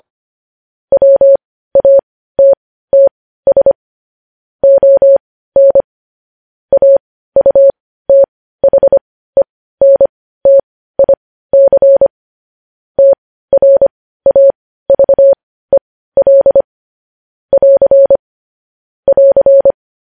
Morse code, heard as a keyed tone, about 25 wpm, spaced as for 15 wpm, 550 Hz. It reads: WATTS ON AUTHENTIC TRAVEL <AR> <AR>
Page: news-headlines-in-morse-code-at-15-wpm